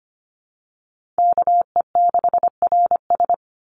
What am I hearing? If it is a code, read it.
Morse code, heard as a keyed tone, about 25 wpm, 700 Hz. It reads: KE6RS
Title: NA SOTA Chaser Callsigns 25wpm